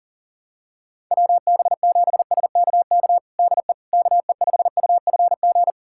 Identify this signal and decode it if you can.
Morse code, heard as a keyed tone, about 37 wpm, 700 Hz. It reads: WB7SKK DE KE5UFG
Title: USA Callsign DE Callsign - 6 Characters - 40wpm